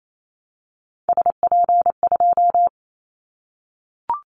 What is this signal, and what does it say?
Morse code, heard as a keyed tone, about 28 wpm, 700 Hz. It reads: SP2
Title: Callsign Prefixes for Top 25 DXCC - 28wpm